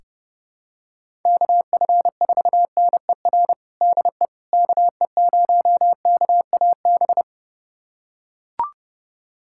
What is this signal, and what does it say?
Morse code, heard as a keyed tone, about 31 wpm, 700 Hz. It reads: KF4NER DE KE0KAB
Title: USA Callsign DE Callsign 30wpm